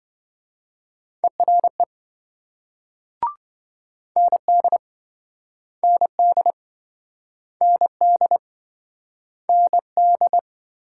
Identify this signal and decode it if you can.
Morse code, heard as a keyed tone, about 26 wpm, 700 Hz. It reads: ERE ND ND ND ND